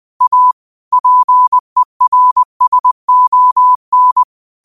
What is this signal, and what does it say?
Morse code, heard as a keyed tone, about 20 wpm, 1000 Hz. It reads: A PERSON
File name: qotdfiles / 20wpm